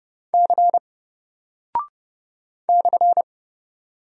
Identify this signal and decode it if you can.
Morse code, heard as a keyed tone, about 30 wpm, 700 Hz. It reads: C /
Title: Single Character (Letter:Number:Punctuation:Prosign) Alt 30wpm